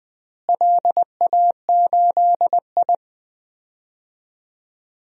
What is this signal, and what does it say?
Morse code, heard as a keyed tone, about 20 wpm, 700 Hz. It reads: LA8I
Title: DXCC-100 Call Signs - 4 Characters - Repeated 2x - 20wpm